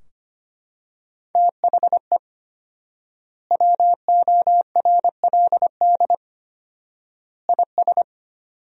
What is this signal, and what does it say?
Morse code, heard as a keyed tone, about 25 wpm, 700 Hz. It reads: THE WORLD IS
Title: 4x Word-Spacing - Sentences from Top 200 Words - 25wpm